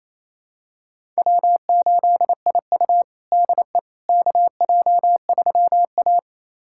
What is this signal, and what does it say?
Morse code, heard as a keyed tone, about 28 wpm, 700 Hz. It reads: W8IU DE KJ3A
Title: USA Callsign DE Callsign - 4 Characters - 28wpm